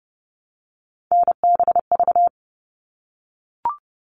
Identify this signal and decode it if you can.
Morse code, heard as a keyed tone, about 30 wpm, 700 Hz. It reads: NBV
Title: All Cut Numbers - 3-Digit 30wpm